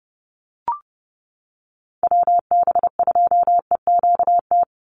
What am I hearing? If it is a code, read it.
Morse code, heard as a keyed tone, about 30 wpm, 700 Hz. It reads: WB2EQT
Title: USA Call Signs 30wpm